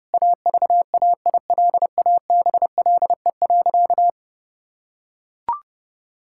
Morse code, heard as a keyed tone, about 30 wpm, 700 Hz. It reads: AVAILABLE.
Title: Sentences from Top 1000 Words - Review Part 6 - 30wpm